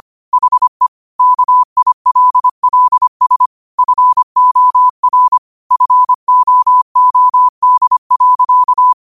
Morse code, heard as a keyed tone, about 25 wpm, 1000 Hz. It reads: HE KILLS FOR FOOD.
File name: 25wpm